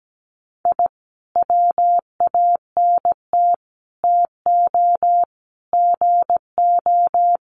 Morse code, heard as a keyed tone, about 17 wpm, 700 Hz. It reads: I WANT TO GO
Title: Sets of 5 Words Encore 17wpm